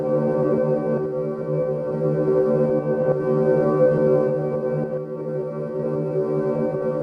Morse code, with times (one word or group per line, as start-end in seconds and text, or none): none